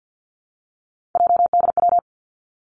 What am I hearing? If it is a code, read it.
Morse code, heard as a keyed tone, about 50 wpm, 700 Hz. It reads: JDW
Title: Three Challenging Letter-Number Combinations 50wpm